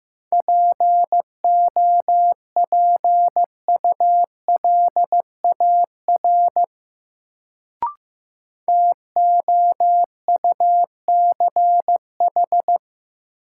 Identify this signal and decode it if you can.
Morse code, heard as a keyed tone, about 15 wpm, 700 Hz. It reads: POPULAR TOUCH
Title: Sentences from Top 1200 Words - New Word Drill Alt - 15wpm